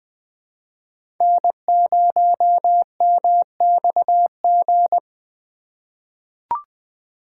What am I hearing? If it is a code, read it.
Morse code, heard as a keyed tone, about 21 wpm, 700 Hz. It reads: N0MXG
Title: USA Call Signs - 3x Character Spacing - 20wpm